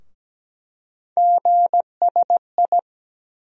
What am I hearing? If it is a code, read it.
Morse code, heard as a keyed tone, about 17 wpm, 700 Hz. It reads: GSI
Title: USA Call Sign Postfixes - 3 Letters 17wpm